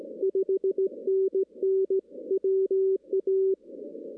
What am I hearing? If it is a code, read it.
Morse code, heard as a keyed tone, about 18 wpm, 380 Hz. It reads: 5NN WA
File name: 2016 arrl dx cw / 2016-ARRL-DX-CW-K5TR- - 0-25